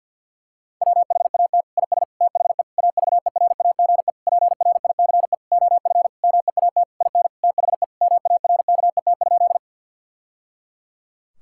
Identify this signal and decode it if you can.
Morse code, heard as a keyed tone, about 50 wpm, 700 Hz. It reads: WHAT IS THE AVERAGE PRICE OF MEAT IN THE MARKET?
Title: Sentences from Top 1000 Words - Review Part 6 - 50wpm